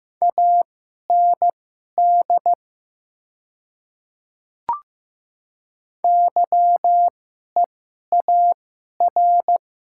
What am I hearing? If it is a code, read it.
Morse code, heard as a keyed tone, about 15 wpm, spaced as for 11 wpm, 700 Hz. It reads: AND YEAR
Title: 2x Character Spacing - Top 200 Words - 15wpm